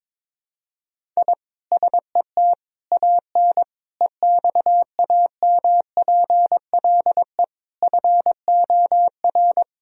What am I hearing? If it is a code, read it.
Morse code, heard as a keyed tone, about 22 wpm, 700 Hz. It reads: I SET AN EXAMPLE FOR
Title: Sentences from Top 1000 Words - Review Part 1 - 22wpm